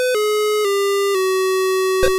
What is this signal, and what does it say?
There is no code.